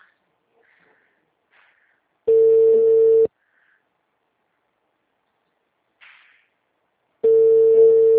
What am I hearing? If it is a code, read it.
Ringback tone (China, 450 Hz).